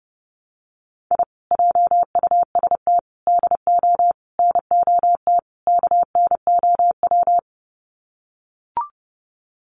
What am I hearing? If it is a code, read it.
Morse code, heard as a keyed tone, about 30 wpm, 700 Hz. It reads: I JUST DO NOT KNOW
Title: Sets of 5 Words Encore 30wpm